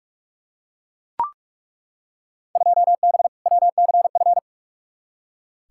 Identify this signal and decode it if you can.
Morse code, heard as a keyed tone, about 42 wpm, 700 Hz. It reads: 2B WCF
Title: Field Day Exchanges Encore - 45wpm